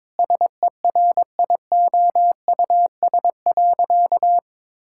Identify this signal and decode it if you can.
Morse code, heard as a keyed tone, about 22 wpm, 700 Hz. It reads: SERIOUS.
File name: practice / 22wpm / Sentences with 3-Word Phrases - 22wpm